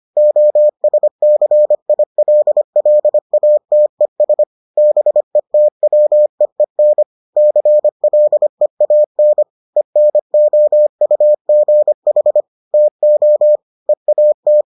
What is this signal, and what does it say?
Morse code, heard as a keyed tone, about 25 wpm, 600 Hz. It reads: OSCILLATES BETWEEN CLEAN ENOUGH TO EAT